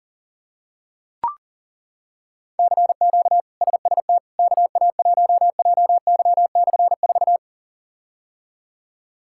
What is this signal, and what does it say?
Morse code, heard as a keyed tone, about 40 wpm, 700 Hz. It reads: CQ SST KA1JY/4
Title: K1USN SST QSO Practice - 40wpm